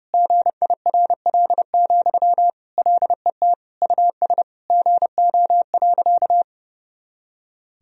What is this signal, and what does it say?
Morse code, heard as a keyed tone, about 30 wpm, 700 Hz. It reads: GIRL, LET US GO.
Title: Sentences from Top 400 Words and Abbreviations 30wpm